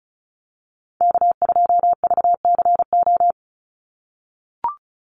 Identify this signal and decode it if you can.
Morse code, heard as a keyed tone, about 34 wpm, 700 Hz. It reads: K2VCO
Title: Call Signs - 35wpm